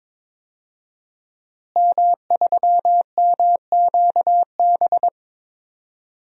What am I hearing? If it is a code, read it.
Morse code, heard as a keyed tone, about 22 wpm, 700 Hz. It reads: M3MQB
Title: DXCC-100 Call Signs - 5 Characters - Repeated 3x - 22wpm